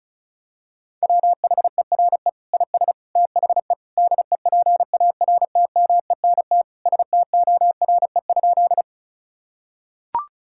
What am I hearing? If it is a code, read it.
Morse code, heard as a keyed tone, about 36 wpm, 700 Hz. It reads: WHERE IS THE DEPARTMENT STORE?
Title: Sentences from Top 1000 Words - Review Part 5 - 35wpm